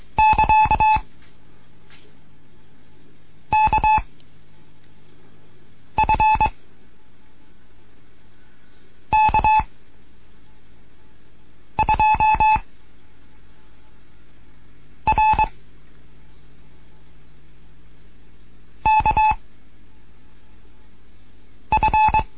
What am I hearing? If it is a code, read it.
Morse code, heard as a keyed tone, about 22 wpm, 860 Hz. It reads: <KA> K F K 2 R K F